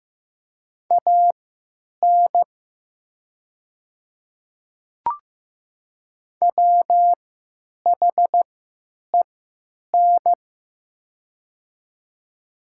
Morse code, heard as a keyed tone, about 15 wpm, spaced as for 9 wpm, 700 Hz. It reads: AN WHEN
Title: 3x Character Spacing - Top 300 Words - 15wpm